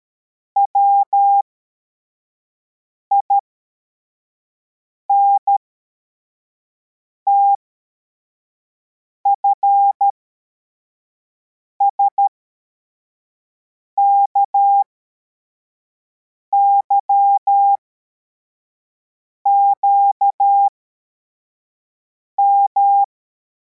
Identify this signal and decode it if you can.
Morse code, heard as a keyed tone, about 13 wpm, spaced as for 4 wpm, 800 Hz. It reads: WINTFSKYQM
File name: formation / cw / Text73